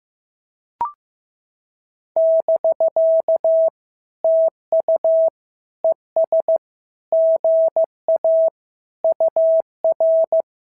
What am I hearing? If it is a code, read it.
Morse code, heard as a keyed tone, about 15 wpm, 650 Hz. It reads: <BK> TU ES GA UR
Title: SOTA QSO Practice 15wpm